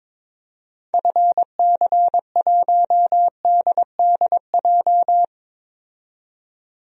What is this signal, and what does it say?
Morse code, heard as a keyed tone, about 22 wpm, 700 Hz. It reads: FC1DDJ
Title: DXCC-100 Call Signs - 6 Characters - 22wpm